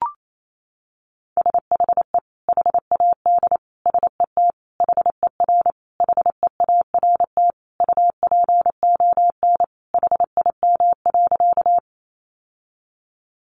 Morse code, heard as a keyed tone, about 28 wpm, 700 Hz. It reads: SHE HAD SET HER HEART UPON HIM.
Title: Sentences from Top 1200 Words 28wpm